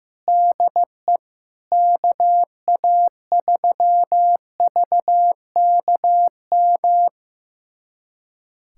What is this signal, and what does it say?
Morse code, heard as a keyed tone, about 15 wpm, 700 Hz. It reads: DE KA3VKM
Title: USA Callsign DE Callsign - 6 Characters - 15wpm